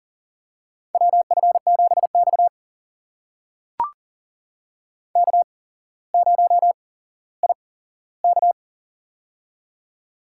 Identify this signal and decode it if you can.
Morse code, heard as a keyed tone, about 40 wpm, spaced as for 17 wpm, 700 Hz. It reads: WF7X K0IK
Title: USA Call Signs - 4 Characters - 8x Character Spacing - 40wpm